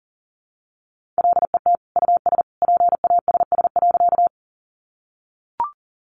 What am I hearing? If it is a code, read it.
Morse code, heard as a keyed tone, about 40 wpm, 700 Hz. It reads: LET US PASS.
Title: Sentences from Top 400 Words 40wpm